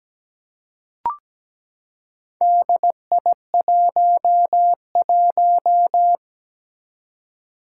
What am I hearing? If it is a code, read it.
Morse code, heard as a keyed tone, about 17 wpm, 700 Hz. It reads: DI11
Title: Four Challenging Letter-Number Combinations 17wpm